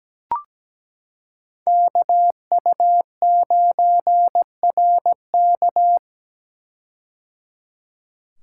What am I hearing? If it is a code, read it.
Morse code, heard as a keyed tone, about 17 wpm, 700 Hz. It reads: KU9RK
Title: K1USN SST QSO Practice - 17wpm